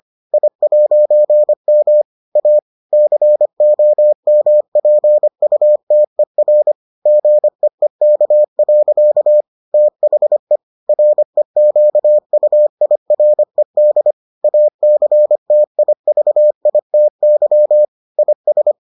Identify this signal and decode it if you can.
Morse code, heard as a keyed tone, about 25 wpm, 600 Hz. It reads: I'M A COMPUTER GEEK. THE REQUIRED ACTIVITY IS